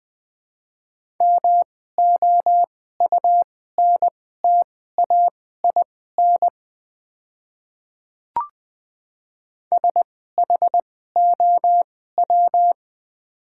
Morse code, heard as a keyed tone, about 20 wpm, spaced as for 14 wpm, 700 Hz. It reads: MOUNTAIN SHOW
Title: 2x Character Spacing - Top 500 Words - 20wpm